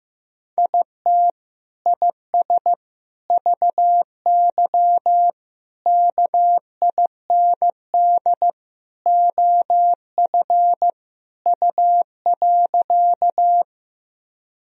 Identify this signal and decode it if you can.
Morse code, heard as a keyed tone, about 16 wpm, 700 Hz. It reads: IT IS VY KIND OF U.
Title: Sentences from Top 200 Words and Abbreviations 15wpm